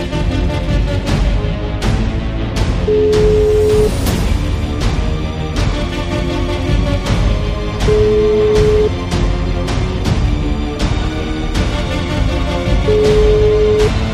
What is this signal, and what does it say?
Ringback tone (Europe, 425 Hz), with noise.